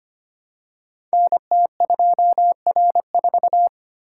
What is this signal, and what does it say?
Morse code, heard as a keyed tone, about 26 wpm, 700 Hz. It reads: NT2R4
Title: Code Groups - 5 Letter:Numbers - 25wpm